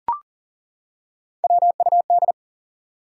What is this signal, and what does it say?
Morse code, heard as a keyed tone, about 42 wpm, 700 Hz. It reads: WUD
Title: Top 180 QSO Elements 40wpm